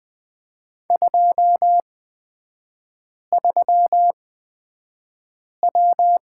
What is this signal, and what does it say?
Morse code, heard as a keyed tone, about 20 wpm, spaced as for 5 wpm, 700 Hz. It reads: 23W
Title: 229 Lesson 37e - Letter x - Three Characters 20wpm